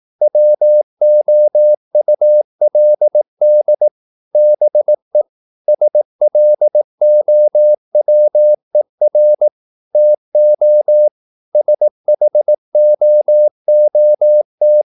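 Morse code, heard as a keyed tone, about 18 wpm, 600 Hz. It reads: WOULD BE SLOWER TO SHOOT
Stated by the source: Morse practice at 18 wpm, English text, from CW med Gnister.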